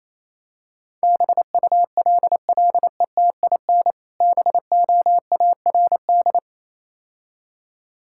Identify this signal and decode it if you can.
Morse code, heard as a keyed tone, about 28 wpm, 700 Hz. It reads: BULLETIN BOARD